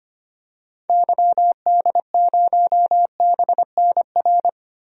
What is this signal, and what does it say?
Morse code, heard as a keyed tone, about 25 wpm, 700 Hz. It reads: YD0BNR
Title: DXCC-100 Call Signs - 6 Characters - Repeated 3x - 25wpm